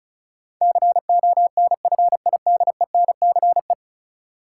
Morse code, heard as a keyed tone, about 34 wpm, 700 Hz. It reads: CONFIDENCE
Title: Sentences from Top 1500 Words - New Word Drill Alt - 35wpm